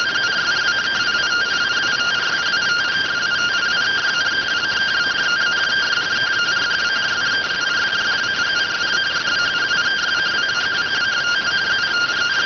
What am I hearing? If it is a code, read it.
Two-tone teleprinter signal, about 170 Hz shift.